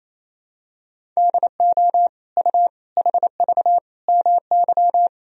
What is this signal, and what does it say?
Morse code, heard as a keyed tone, about 28 wpm, 700 Hz. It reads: DO U HV MY
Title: Sentences from Top 500 Words and Abbreviations 28wpm